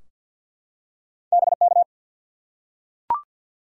Morse code, heard as a keyed tone, about 52 wpm, 700 Hz. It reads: BK
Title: Top 90 QSO Elements 50wpm